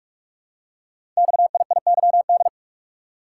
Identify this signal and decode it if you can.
Morse code, heard as a keyed tone, about 44 wpm, 700 Hz. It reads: XIIYD
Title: Code Groups - 5 Letter:Numbers - 45wpm